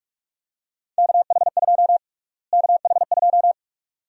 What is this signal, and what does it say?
Morse code, heard as a keyed tone, about 46 wpm, 700 Hz. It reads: KH2 KH2
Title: USA Call Sign Prefixes - Repeated 2x - 45wpm